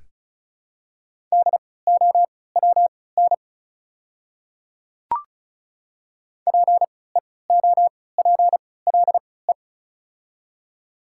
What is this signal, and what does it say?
Morse code, heard as a keyed tone, about 36 wpm, spaced as for 20 wpm, 700 Hz. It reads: DOWN PEOPLE
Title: 3x Character Spacing - Top 200 Words - 35wpm